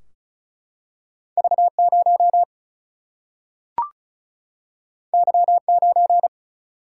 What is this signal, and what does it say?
Morse code, heard as a keyed tone, about 36 wpm, 700 Hz. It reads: V0 Y9
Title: Single-Letter Single-Number 35wpm